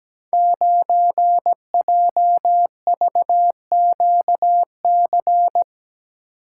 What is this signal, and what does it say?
Morse code, heard as a keyed tone, about 17 wpm, 700 Hz. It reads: 9JVQC
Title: Code Groups - 5 Letter:Numbers - 17wpm